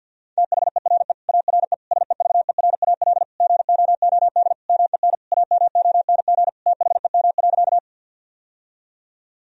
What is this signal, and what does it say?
Morse code, heard as a keyed tone, about 50 wpm, 700 Hz. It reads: THERE ARE SEVERAL GOOD MEN AMONG THEM.
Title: Sentences from Top 1000 Words - Review Part 2 - 50wpm